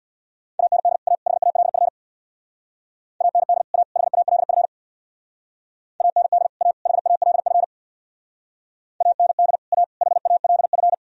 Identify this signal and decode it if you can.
Morse code, heard as a keyed tone, about 60 wpm, 700 Hz. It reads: AND A HALF AND A HALF AND A HALF AND A HALF
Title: Speed-Racing - Sets of 3 Words 50wpm